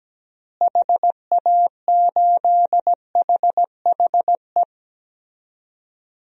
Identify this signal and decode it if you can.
Morse code, heard as a keyed tone, about 17 wpm, 700 Hz. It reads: HA8HHE
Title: DXCC-100 Call Signs - 6 Characters - Repeated 2x - 17wpm